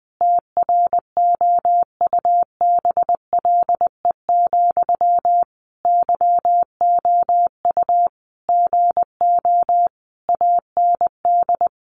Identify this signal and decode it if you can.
Morse code, heard as a keyed tone, about 20 wpm, 700 Hz. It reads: TROUBLE, YOU GO AND
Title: Sentences from Top 1200 Words 20wpm